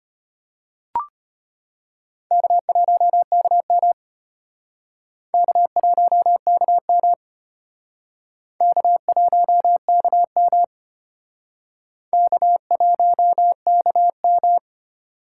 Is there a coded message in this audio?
Yes: Morse code from 2.3 to 14.6 s, heard as a keyed tone, about 33 wpm, 700 Hz. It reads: K1KM K1KM K1KM K1KM